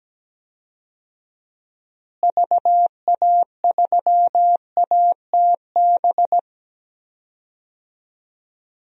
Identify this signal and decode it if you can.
Morse code, heard as a keyed tone, about 17 wpm, 700 Hz. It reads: VA3ATB
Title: DXCC-100 Call Signs - 6 Characters - Repeated 3x - 17wpm